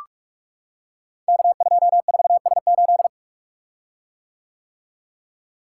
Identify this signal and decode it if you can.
Morse code, heard as a keyed tone, about 44 wpm, 700 Hz. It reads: K24S8
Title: Code Groups - 5 Letter:Numbers - 45wpm